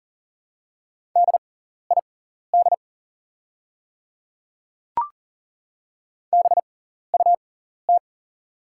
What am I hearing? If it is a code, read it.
Morse code, heard as a keyed tone, about 40 wpm, spaced as for 14 wpm, 700 Hz. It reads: DID BUT
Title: 6x Character Spacing - Top 100 Words - 40wpm